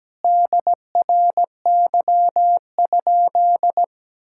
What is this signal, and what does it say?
Morse code, heard as a keyed tone, about 17 wpm, 700 Hz. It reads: DRY?
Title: Sentences from Top 1600 Words 17wpm